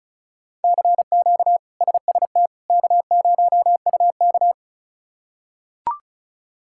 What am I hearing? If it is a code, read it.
Morse code, heard as a keyed tone, about 34 wpm, 700 Hz. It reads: CQ SST K0UK
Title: K1USN SST QSO Practice - 35wpm